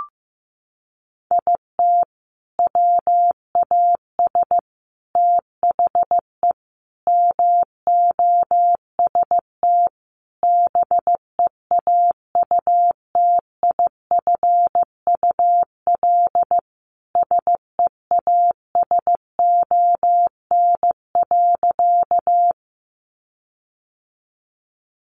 Morse code, heard as a keyed tone, about 15 wpm, 700 Hz. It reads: IT WAS THE MOST BEAUTIFUL SEASON.
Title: Sentences from Top 1400 Words 15wpm